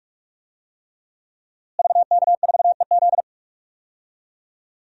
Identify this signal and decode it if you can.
Morse code, heard as a keyed tone, about 46 wpm, 700 Hz. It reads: VK4EZ
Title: DXCC-100 Call Signs - 5 Characters - Repeated 3x - 45wpm